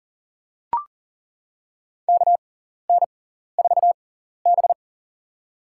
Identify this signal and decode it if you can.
Morse code, heard as a keyed tone, about 37 wpm, spaced as for 14 wpm, 700 Hz. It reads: KN4B
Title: USA Call Signs - 4 Characters - 6x Character Spacing - 40wpm